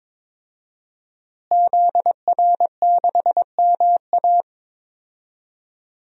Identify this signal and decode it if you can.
Morse code, heard as a keyed tone, about 21 wpm, 700 Hz. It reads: ZR6MA
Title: DXCC-100 Call Signs - 5 Characters - Repeated 3x - 22wpm